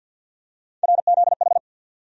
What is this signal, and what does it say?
Morse code, heard as a keyed tone, about 52 wpm, 700 Hz. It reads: RZH
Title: USA Call Sign Postfixes - 3 Letters 50wpm